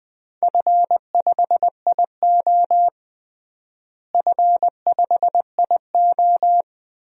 Morse code, heard as a keyed tone, about 21 wpm, 700 Hz. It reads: F5IO F5IO
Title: DXCC-100 Call Signs - 4 Characters - Repeated 3x - 20wpm